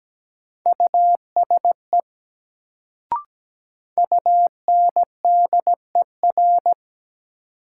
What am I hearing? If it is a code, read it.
Morse code, heard as a keyed tone, about 17 wpm, 700 Hz. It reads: USE UNDER
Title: Sets of 2 Words - New Word Drill - 17wpm